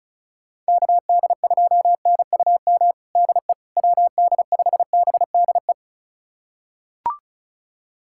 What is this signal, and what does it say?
Morse code, heard as a keyed tone, about 36 wpm, 700 Hz. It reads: KD2NUM DE WD5BDE
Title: USA Callsign DE Callsign 35wpm